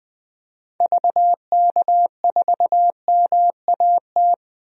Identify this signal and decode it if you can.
Morse code, heard as a keyed tone, about 20 wpm, 700 Hz. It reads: VK4MAT
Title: DXCC-100 Call Signs - 6 Characters - 20wpm